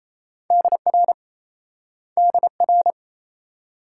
Morse code, heard as a keyed tone, about 37 wpm, 700 Hz. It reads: DR DR